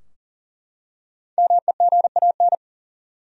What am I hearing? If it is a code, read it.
Morse code, heard as a keyed tone, about 40 wpm, 700 Hz. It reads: MEGAN
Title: USA Common Names - Max 5 Letters - 40wpm